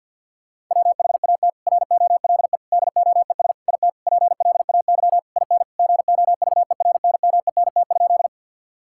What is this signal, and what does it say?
Morse code, heard as a keyed tone, about 50 wpm, 700 Hz. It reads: WHAT ROLE DOES IT PLAY IN GOVERNMENT?
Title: Sentences from Top 1000 Words - Review Part 6 - 50wpm